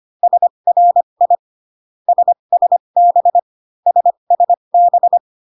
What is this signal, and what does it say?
Morse code, heard as a keyed tone, about 25 wpm, spaced as for 22 wpm, 700 Hz. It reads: SRI SSB SSB
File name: CW common abbreviations alphabetical repeated 3 times 25 18wpm